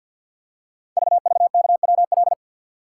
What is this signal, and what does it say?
Morse code, heard as a keyed tone, about 46 wpm, 700 Hz. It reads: VVKWL
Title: Code Groups - 5 Letters - 50wpm